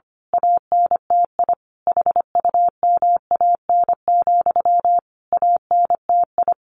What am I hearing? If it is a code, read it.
Morse code, heard as a keyed tone, about 25 wpm, 700 Hz. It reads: ANTI HUMAN, ANTI
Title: A Sound Of Thunder 25wpm